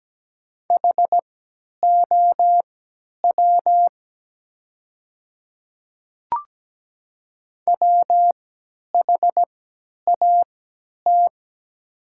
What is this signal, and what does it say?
Morse code, heard as a keyed tone, about 17 wpm, spaced as for 10 wpm, 700 Hz. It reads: HOW WHAT
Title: 3x Character Spacing - Top 300 Words - 17wpm